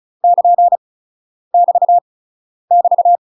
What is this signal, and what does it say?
Morse code, heard as a keyed tone, about 34 wpm, 700 Hz. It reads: <KN> <BT> <BT>
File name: UK MASTER QSOs TRAINING / 35WPM / UK MASTER QSOs 0024